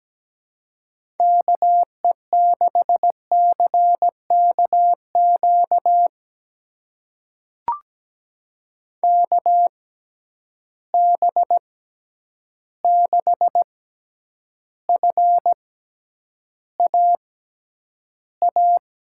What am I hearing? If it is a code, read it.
Morse code, heard as a keyed tone, about 17 wpm, spaced as for 9 wpm, 700 Hz. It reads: KE6CKQ KB6FAA
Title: USA Call Signs - 6 Characters - 6x Character Spacing - 17wpm